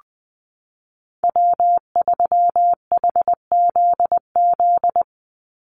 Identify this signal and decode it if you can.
Morse code, heard as a keyed tone, about 20 wpm, 700 Hz. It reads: W3HZZ
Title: Call Signs Repeated 3x - 20wpm